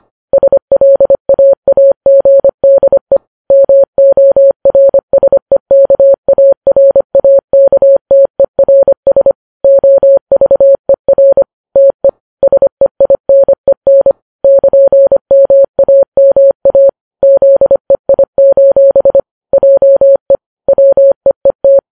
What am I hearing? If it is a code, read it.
Morse code, heard as a keyed tone, about 25 wpm, 550 Hz. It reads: SLAAGDE MORSEKARAKTERS OVER TE SEINEN <KN>MAMA ZEI: JE WEET
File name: Files Web tls / Morse on9fv / morse25kps